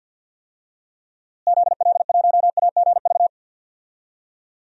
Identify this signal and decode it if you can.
Morse code, heard as a keyed tone, about 48 wpm, 700 Hz. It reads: ZL1AGV
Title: DXCC-100 Call Signs - 6 Characters - Repeated 3x - 50wpm